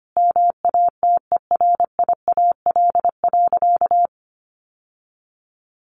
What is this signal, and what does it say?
Morse code, heard as a keyed tone, about 25 wpm, 700 Hz. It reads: MATERIAL.
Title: Sentences from Top 1100 Words 25wpm